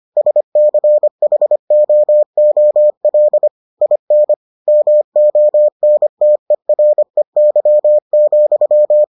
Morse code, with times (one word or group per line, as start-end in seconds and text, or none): 0.2-3.5 SCHOOL
3.8-4.3 IN
4.7-9.0 MONTEREY,